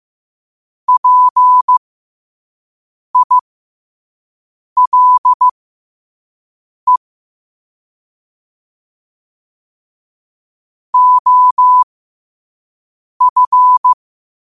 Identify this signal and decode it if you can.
Morse code, heard as a keyed tone, about 15 wpm, spaced as for 5 wpm, 1000 Hz. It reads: PILE OF
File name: qotdfiles / 5wpm